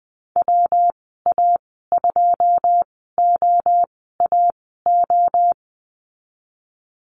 Morse code, heard as a keyed tone, about 20 wpm, spaced as for 15 wpm, 700 Hz. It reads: WA2OAO
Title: 122 Lesson 20g - Number 3 - Call Signs 20wpm